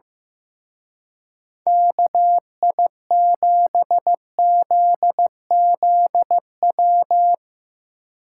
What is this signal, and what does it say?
Morse code, heard as a keyed tone, about 15 wpm, 700 Hz. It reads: KI7ZZW
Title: Speed Racing - USA Call Signs 15wpm